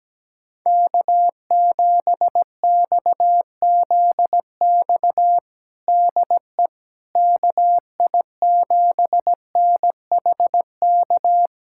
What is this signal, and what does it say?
Morse code, heard as a keyed tone, about 17 wpm, 700 Hz. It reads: K7XZX DE KI7NHK
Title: USA Callsign DE Callsign 17wpm